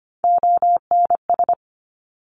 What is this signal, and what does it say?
Morse code, heard as a keyed tone, about 25 wpm, 700 Hz. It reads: ONS
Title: Top Three Letter Combinations in Words 25wpm